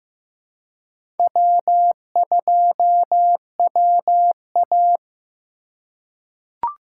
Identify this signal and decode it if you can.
Morse code, heard as a keyed tone, about 15 wpm, 700 Hz. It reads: W2WA
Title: USA Call Signs - 4 Characters - 2x Character Spacing - 15wpm